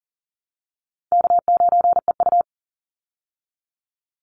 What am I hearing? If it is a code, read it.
Morse code, heard as a keyed tone, about 40 wpm, 700 Hz. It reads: K9EU
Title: Call Signs Repeated 3x - 40wpm